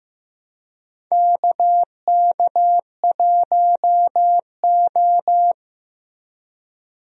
Morse code, heard as a keyed tone, about 15 wpm, 700 Hz. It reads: KK1O
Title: USA Call Signs - 4 Characters - Repeated 3x - 15wpm